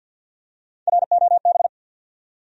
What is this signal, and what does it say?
Morse code, heard as a keyed tone, about 52 wpm, 700 Hz. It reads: ROB
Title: USA Common Names - Max 3 Letters - 50wpm